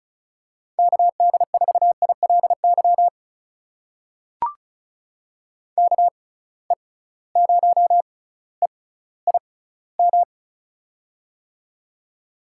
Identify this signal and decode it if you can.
Morse code, heard as a keyed tone, about 36 wpm, spaced as for 18 wpm, 700 Hz. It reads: KD4ILY KE0EIM
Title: USA Call Signs - 6 Characters - 6x Character Spacing - 35wpm